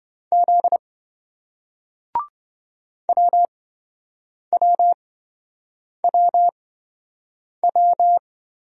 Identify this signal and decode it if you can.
Morse code, heard as a keyed tone, about 26 wpm, spaced as for 7 wpm, 700 Hz. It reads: Z WWWW